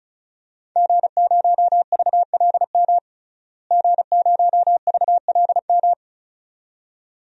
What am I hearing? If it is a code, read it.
Morse code, heard as a keyed tone, about 34 wpm, 700 Hz. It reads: G0VLM G0VLM
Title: DXCC-100 Call Signs - 5 Characters - Repeated 2x - 35wpm